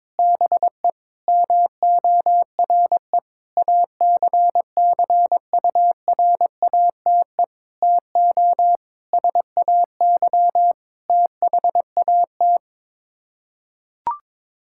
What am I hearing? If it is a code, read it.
Morse code, heard as a keyed tone, about 22 wpm, 700 Hz. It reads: BE MORE ACCURATE TO SAY THAT
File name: practice / 22wpm / Sets of 8 Words - 22wpm